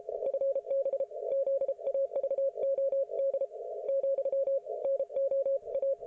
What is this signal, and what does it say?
Morse code, heard as a keyed tone, about 33 wpm, 550 Hz. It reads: FB ZAVOD , NOA